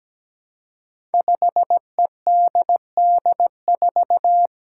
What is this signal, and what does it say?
Morse code, heard as a keyed tone, about 17 wpm, 700 Hz. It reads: 5EDD4
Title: Five Challenging Letter-Number Combinations 17wpm